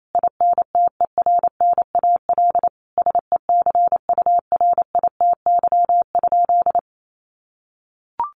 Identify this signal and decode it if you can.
Morse code, heard as a keyed tone, about 28 wpm, 700 Hz. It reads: INTERNAL SECURITY?
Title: Sentences from Top 1400 Words 28wpm